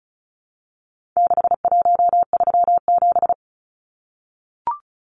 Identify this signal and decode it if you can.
Morse code, heard as a keyed tone, about 34 wpm, 700 Hz. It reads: 6137
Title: Numbers - 4-Digit 35wpm